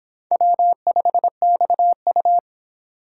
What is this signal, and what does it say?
Morse code, heard as a keyed tone, about 26 wpm, 700 Hz. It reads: W5XU